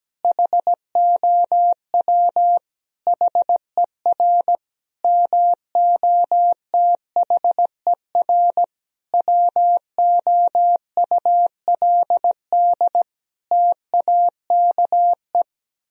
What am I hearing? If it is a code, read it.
Morse code, heard as a keyed tone, about 17 wpm, 700 Hz. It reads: HOW HER MOTHER WOULD TAKE
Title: Sentences from Top 1000 Words - Review Part 4 - 17wpm